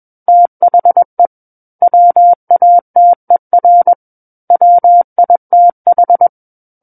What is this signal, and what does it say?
Morse code, heard as a keyed tone, about 21 wpm, 700 Hz. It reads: THE WATER WITH
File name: Book One / 12-Chapter 12 - 21 WPM